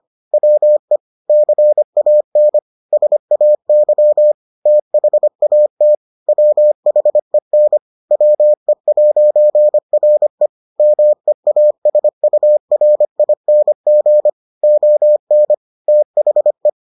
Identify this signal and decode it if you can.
Morse code, heard as a keyed tone, about 25 wpm, 600 Hz. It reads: WE CAN SAY THAT WHEN WE'RE MEASURING ON THE